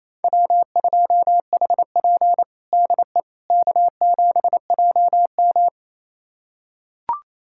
Morse code, heard as a keyed tone, about 27 wpm, 690 Hz. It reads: W2HP DE K7JM
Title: USA Callsign DE Callsign - 4 Characters - 28wpm